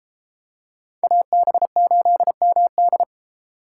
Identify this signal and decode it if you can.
Morse code, heard as a keyed tone, about 31 wpm, 700 Hz. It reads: AB8MD